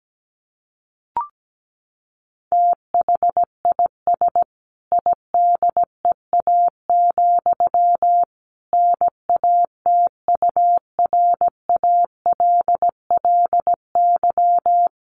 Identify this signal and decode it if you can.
Morse code, heard as a keyed tone, about 17 wpm, 700 Hz. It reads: THIS IDEA, NATURALLY
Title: Sentences from Top 1300 Words 17wpm